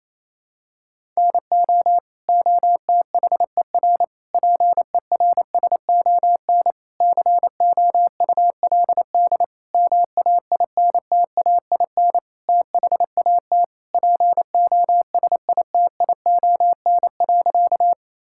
Morse code, heard as a keyed tone, about 28 wpm, 700 Hz. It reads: NO OTHER PERSON COULD MAINTAIN THAT POSITION.
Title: Sentences from Top 1000 Words - Review Part 5 - 28wpm